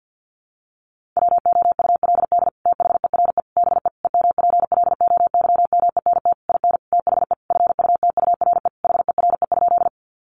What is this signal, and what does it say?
Morse code, heard as a keyed tone, about 50 wpm, 700 Hz. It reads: WOULD THERE BE EMPLOYMENT IN THE FUTURE HERE?
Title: Sentences from Top 1500 Words 50wpm